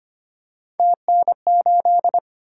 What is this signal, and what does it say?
Morse code, heard as a keyed tone, about 25 wpm, 700 Hz. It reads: TN8
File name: Three Letter-Number Combinations - 25wpm